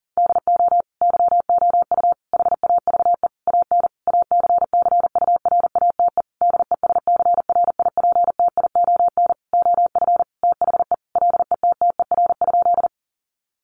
Morse code, heard as a keyed tone, about 40 wpm, 700 Hz. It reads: DO YOU HAVE AN ACCURATE DESCRIPTION OF THE LETTER?
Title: Sentences from Top 1300 Words 40wpm